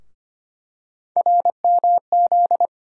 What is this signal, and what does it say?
Morse code, heard as a keyed tone, about 25 wpm, 700 Hz. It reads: RMZ
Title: USA Call Sign Postfixes - 3 Letters 25wpm